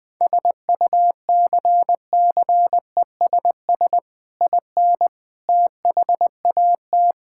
Morse code, heard as a keyed tone, about 20 wpm, 700 Hz. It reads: SUCCESS IN THAT
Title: Sentences from Top 1000 Words - Review Part 5 - 20wpm